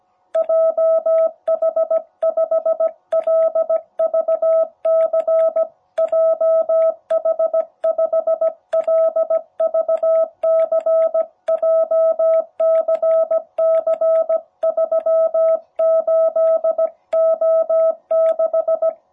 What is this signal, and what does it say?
Morse code, heard as a keyed tone, about 17 wpm, 660 Hz. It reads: JH5LVC JH5LVC JCC38O6
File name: jh8cbh / 2021cw